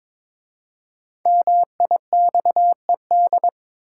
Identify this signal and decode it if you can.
Morse code, heard as a keyed tone, about 21 wpm, 700 Hz. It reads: MIXED
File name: 5 Letter Words - First 3 Letters Given - 22wpm